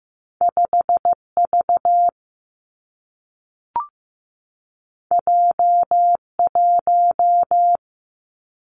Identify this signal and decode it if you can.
Morse code, heard as a keyed tone, about 15 wpm, 700 Hz. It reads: 5V J1
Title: Two Challenging Letter-Number Combinations 15wpm